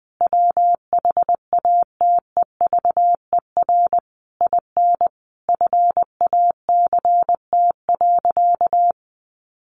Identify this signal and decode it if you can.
Morse code, heard as a keyed tone, about 20 wpm, 700 Hz. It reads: WHATEVER IN FACT.
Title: Sentences from Top 1200 Words 20wpm